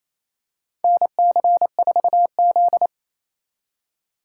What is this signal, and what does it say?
Morse code, heard as a keyed tone, about 28 wpm, 700 Hz. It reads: NC4Z
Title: USA Call Signs - 2x Character Spacing - 28wpm